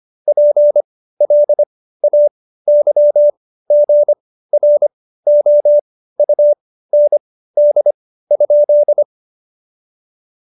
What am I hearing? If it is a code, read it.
Morse code, heard as a keyed tone, about 25 wpm, spaced as for 15 wpm, 590 Hz. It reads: PLAYGROUND?
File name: Jokes-25-char-15-eff